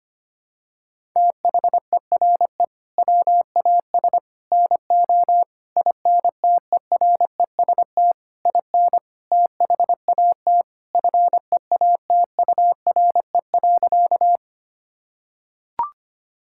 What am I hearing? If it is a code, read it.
Morse code, heard as a keyed tone, about 25 wpm, 700 Hz. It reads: THERE WAS NO INTEREST IN THAT FEATURE.
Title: Sentences from Top 1000 Words - Review Part 5 - 25wpm